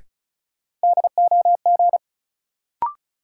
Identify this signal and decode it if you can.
Morse code, heard as a keyed tone, about 36 wpm, 700 Hz. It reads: DOG
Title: Sentences from Top 400 Words - New Word Drill - 35wpm